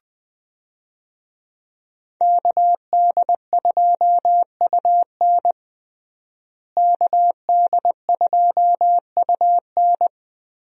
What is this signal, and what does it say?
Morse code, heard as a keyed tone, about 20 wpm, 700 Hz. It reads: KD2UN KD2UN
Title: USA Call Signs - 5 Characters - Repeated 3x - 20wpm